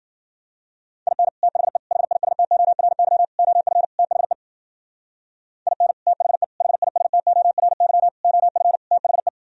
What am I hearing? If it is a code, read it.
Morse code, heard as a keyed tone, about 46 wpm, 700 Hz. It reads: IN THE HISTORY OF THE IN THE HISTORY OF THE
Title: Speed-Racing - Sets of 5 Words 40wpm